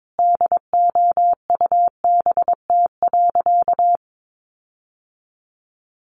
Morse code, heard as a keyed tone, about 22 wpm, 700 Hz. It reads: DOUBT.
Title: Sentences from Top 1500 Words 22wpm